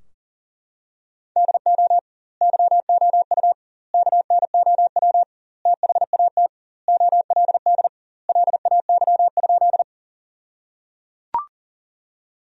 Morse code, heard as a keyed tone, about 37 wpm, 700 Hz. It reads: DO YOU KNOW THAT OLD LAY?
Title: 2x Word-Spacing - Sentences from Top 400 Words - 40wpm